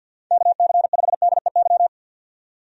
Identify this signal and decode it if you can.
Morse code, heard as a keyed tone, about 51 wpm, 700 Hz. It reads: KC5DEY
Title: USA Call Signs - 6 Characters - 50wpm